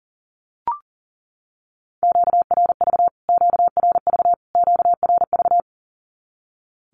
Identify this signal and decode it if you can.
Morse code, heard as a keyed tone, about 40 wpm, 700 Hz. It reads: QRV QRV QRV
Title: Q-Codes Repeated 3x 40wpm